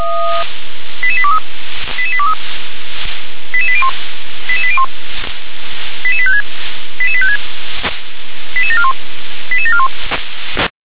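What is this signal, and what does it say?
Five-tone selective call (ZVEI), heard as quick runs of five tones, tones of about 70 ms. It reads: 88823 88823 88801 88801 88856 88856 88741 88741
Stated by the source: ZVEI five-tone selective call received on SDR.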